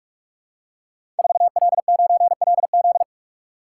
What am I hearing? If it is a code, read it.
Morse code, heard as a keyed tone, about 44 wpm, 700 Hz. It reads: 4L9LZ
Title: Code Groups - 5 Letter:Numbers - 45wpm